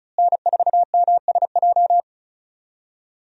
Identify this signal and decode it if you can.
Morse code, heard as a keyed tone, about 36 wpm, 700 Hz. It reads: N4MSJ
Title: USA Call Signs - 5 Characters - Repeated 2x - 35wpm